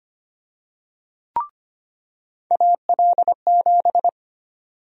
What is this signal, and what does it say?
Morse code, heard as a keyed tone, about 25 wpm, 700 Hz. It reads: AL7
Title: USA Call Sign Prefixes - 25wpm